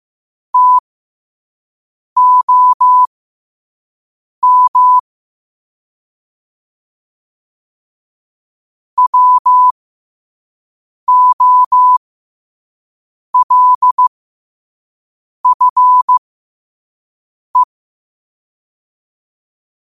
Morse code, heard as a keyed tone, about 15 wpm, spaced as for 5 wpm, 1000 Hz. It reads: TOM WOLFE
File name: qotdfiles / 5wpm